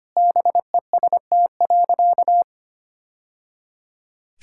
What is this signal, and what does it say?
Morse code, heard as a keyed tone, about 25 wpm, 690 Hz. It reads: BEST.
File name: practice / 25wpm / Sentences with 2-Word Phrases - 25wpm